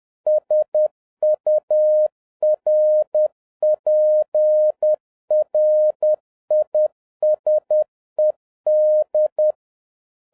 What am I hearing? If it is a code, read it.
Morse code, heard as a keyed tone, about 10 wpm, 610 Hz. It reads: SURPRISED